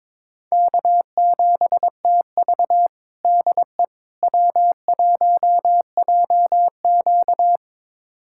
Morse code, heard as a keyed tone, about 21 wpm, 700 Hz. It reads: K7TV DE W1JQ
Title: USA Callsign DE Callsign - 4 Characters - 22wpm